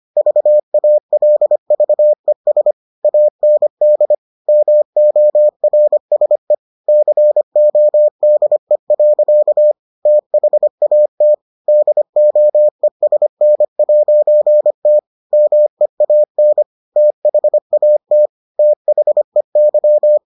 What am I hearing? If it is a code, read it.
Morse code, heard as a keyed tone, about 25 wpm, 600 Hz. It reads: VALVES AND MORSE CODE. THAT DOESN'T MEAN THAT THEY